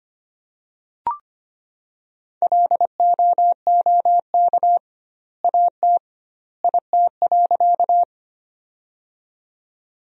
Morse code, heard as a keyed tone, about 25 wpm, 700 Hz. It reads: LOOK AT IT.
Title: 2x Word-Spacing - Sentences from Top 100 Words - 25wpm